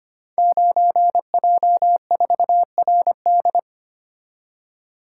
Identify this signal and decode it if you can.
Morse code, heard as a keyed tone, about 25 wpm, 700 Hz. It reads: 9J4RD
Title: Code Groups - 5 Letter:Numbers - 25wpm